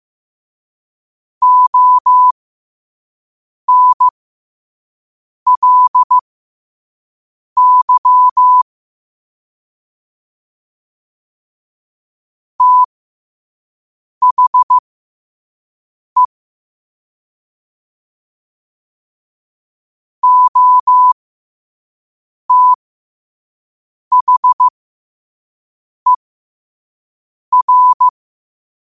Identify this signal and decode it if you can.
Morse code, heard as a keyed tone, about 15 wpm, spaced as for 5 wpm, 1000 Hz. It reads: ONLY THE OTHER
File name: qotdfiles / 5wpm